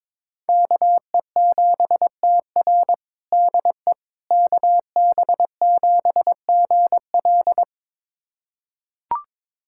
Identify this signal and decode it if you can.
Morse code, heard as a keyed tone, about 22 wpm, 700 Hz. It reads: KE7TR DE KB7GL
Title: USA Callsign DE Callsign - 5 Characters - 22wpm